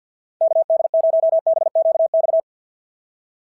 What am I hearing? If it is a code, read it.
Morse code, heard as a keyed tone, about 47 wpm, 650 Hz. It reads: KD0BQX
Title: SOTA QSO Practice 50wpm